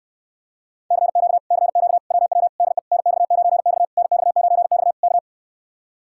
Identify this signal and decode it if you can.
Morse code, heard as a keyed tone, about 66 wpm, 700 Hz. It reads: CQ CQ WW DE NX9X NX9X K